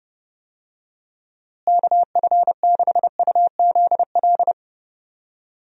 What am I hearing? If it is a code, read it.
Morse code, heard as a keyed tone, about 30 wpm, 700 Hz. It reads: KF6UZL
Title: USA Call Signs - 6 Characters - Repeated 2x - 30wpm